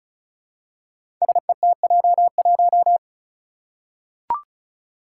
Morse code, heard as a keyed tone, about 36 wpm, 700 Hz. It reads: SETJ1
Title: Five Challenging Letter-Number Combinations 35wpm